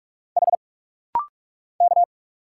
Morse code, heard as a keyed tone, about 44 wpm, 700 Hz. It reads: H K
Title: Single Letter-Number - Rapid-Fire 45wpm